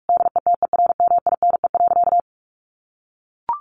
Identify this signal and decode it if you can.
Morse code, heard as a keyed tone, about 44 wpm, 700 Hz. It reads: DETERMINE.
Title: Sentences from Top 1500 Words 45wpm